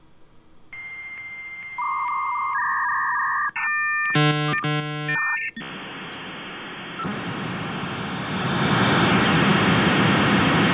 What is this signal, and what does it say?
Dial-up modem handshake.